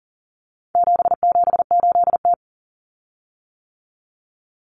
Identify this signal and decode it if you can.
Morse code, heard as a keyed tone, about 40 wpm, 700 Hz. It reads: 778T
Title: Common Cut Numbers - 4-Digit 40wpm